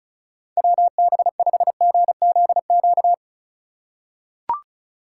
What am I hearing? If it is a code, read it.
Morse code, heard as a keyed tone, about 36 wpm, 700 Hz. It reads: WB5GZQ
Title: USA Call Signs - 6 Characters - 2x Character Spacing - 35wpm